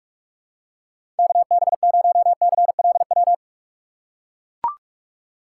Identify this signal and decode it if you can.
Morse code, heard as a keyed tone, about 43 wpm, 700 Hz. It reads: KB0CLW
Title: USA Call Signs - 6 Characters - 2x Character Spacing - 45wpm